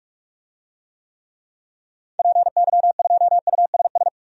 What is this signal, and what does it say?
Morse code, heard as a keyed tone, about 44 wpm, 700 Hz. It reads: PY2USS
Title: DXCC-100 Call Signs - 6 Characters - 45wpm